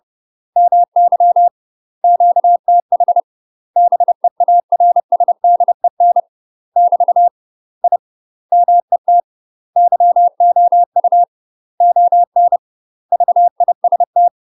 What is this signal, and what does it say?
Morse code, heard as a keyed tone, about 30 wpm, 700 Hz. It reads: MY QTH BEARSDEN <BT> I MET YOU ON VIST
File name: UK MASTER QSOs TRAINING / 30WPM / UK MASTER QSOs 0011